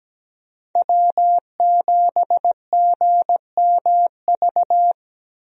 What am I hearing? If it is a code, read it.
Morse code, heard as a keyed tone, about 17 wpm, 700 Hz. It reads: W7GMV
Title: USA Call Signs - 5 Characters - Repeated 2x - 17wpm